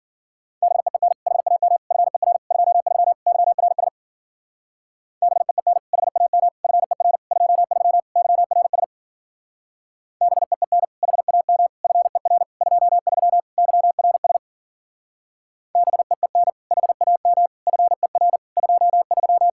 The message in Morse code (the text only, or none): BEEN HAM FER 23 YRS BEEN HAM FER 23 YRS BEEN HAM FER 23 YRS BEEN HAM FER 23